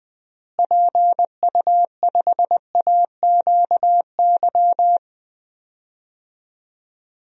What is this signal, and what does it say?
Morse code, heard as a keyed tone, about 21 wpm, 700 Hz. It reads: PU5AQY
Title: DXCC-100 Call Signs - 6 Characters - Repeated 2x - 20wpm